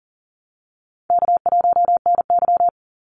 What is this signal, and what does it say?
Morse code, heard as a keyed tone, about 40 wpm, 700 Hz. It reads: K1NY
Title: Easy Call Signs - 40wpm